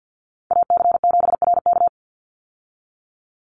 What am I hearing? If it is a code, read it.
Morse code, heard as a keyed tone, about 50 wpm, 700 Hz. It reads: AC7RK
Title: USA Call Signs 50wpm